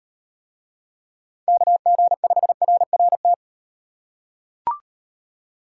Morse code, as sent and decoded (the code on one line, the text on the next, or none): -.- --. ..... .-. .-. -
KG5RRT